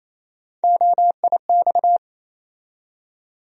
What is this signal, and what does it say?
Morse code, heard as a keyed tone, about 28 wpm, 700 Hz. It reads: OIX
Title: USA Call Sign Postfixes - 3 Letters 28wpm